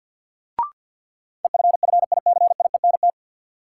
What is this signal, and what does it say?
Morse code, heard as a keyed tone, about 53 wpm, 700 Hz. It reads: EFFICIENT
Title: Sentences from Top 1300 Words - New Word Drill Alt - 50wpm